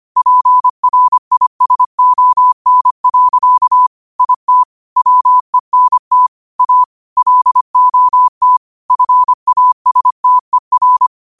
Morse code, heard as a keyed tone, about 25 wpm, 1000 Hz. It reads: PRISON. IT WENT A LOT FASTER
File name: qotdfiles / 25wpm